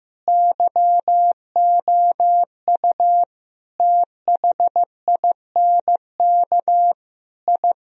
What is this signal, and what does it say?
Morse code, heard as a keyed tone, about 14 wpm, 700 Hz. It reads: YOU THINK I
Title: Sentences from Top 1000 Words - Review Part 3 - 15wpm